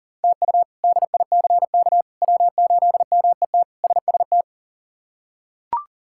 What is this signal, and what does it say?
Morse code, heard as a keyed tone, about 40 wpm, 700 Hz. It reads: TU DICK W8MET SST
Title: K1USN SST QSO Practice - 40wpm